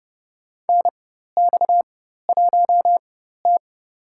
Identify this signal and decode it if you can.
Morse code, heard as a keyed tone, about 30 wpm, spaced as for 14 wpm, 700 Hz. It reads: NX1T
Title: USA Call Signs - 4 Characters - 4x Character Spacing - 30wpm